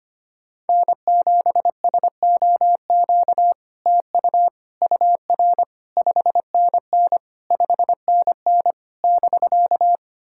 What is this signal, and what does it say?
Morse code, heard as a keyed tone, about 25 wpm, 700 Hz. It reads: N7SOQ TU UR 5NN 5NN <BK>
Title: SOTA QSO Practice 25wpm